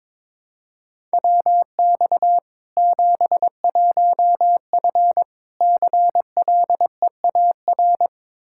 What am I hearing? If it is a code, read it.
Morse code, heard as a keyed tone, about 22 wpm, 700 Hz. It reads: WX 71F CLEAR
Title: K1USN SST QSO Practice 22wpm